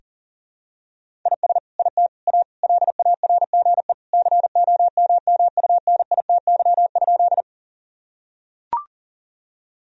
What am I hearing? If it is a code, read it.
Morse code, heard as a keyed tone, about 40 wpm, 700 Hz. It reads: IS IT A LARGE COMMUNITY?
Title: Sentences from Top 1000 Words - Review Part 4 - 40wpm